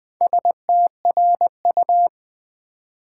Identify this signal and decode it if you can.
Morse code, heard as a keyed tone, about 20 wpm, 700 Hz. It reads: STRU
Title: Build A Word - 4 Letters At A Time - 10-Letter Words - 20wpm